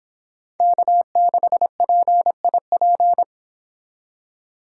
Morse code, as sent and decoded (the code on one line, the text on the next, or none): -.- -.... .--. .. .--.
K6PIP